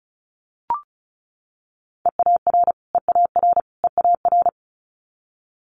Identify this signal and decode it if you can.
Morse code, heard as a keyed tone, about 36 wpm, 700 Hz. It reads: EAR EAR EAR
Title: Multi-letter Phonemes Repeated 3x 35wpm